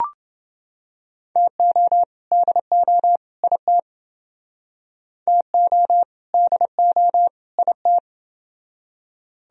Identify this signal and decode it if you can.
Morse code, heard as a keyed tone, about 29 wpm, 700 Hz. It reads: TO DO IT TO DO IT